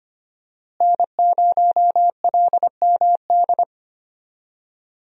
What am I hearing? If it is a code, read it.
Morse code, heard as a keyed tone, about 25 wpm, 700 Hz. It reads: N0LMD
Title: USA Call Signs - 5 Characters - Repeated 2x - 25wpm